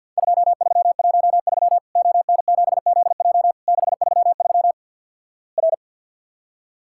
Morse code, heard as a keyed tone, about 50 wpm, 700 Hz. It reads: 2313 ON7ZJ 633 R
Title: Worked All Europe DX Contest - QTC Exchange - 50wpm